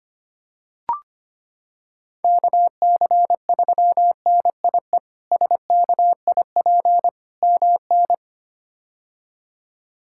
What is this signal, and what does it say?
Morse code, heard as a keyed tone, about 25 wpm, 700 Hz. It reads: KC3NIE SKIP MN
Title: K1USN SST QSO Practice - 25wpm